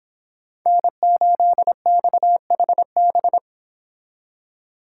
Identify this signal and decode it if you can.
Morse code, heard as a keyed tone, about 31 wpm, 700 Hz. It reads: N8XHB